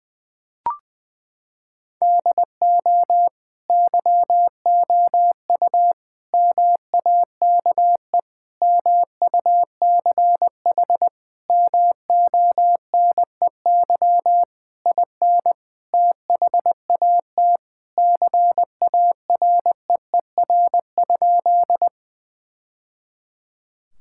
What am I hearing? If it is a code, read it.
Morse code, heard as a keyed tone, about 21 wpm, 700 Hz. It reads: DO YOU MAKE MUCH MONEY IN THAT CAREER?
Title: Sentences from Top 1000 Words - Review Part 6 - 20wpm